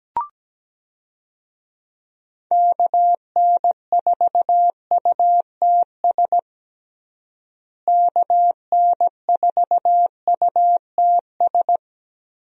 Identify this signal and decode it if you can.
Morse code, heard as a keyed tone, about 17 wpm, 700 Hz. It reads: KN4UTS KN4UTS
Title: USA Call Signs - 6 Characters - Repeated 3x - 17wpm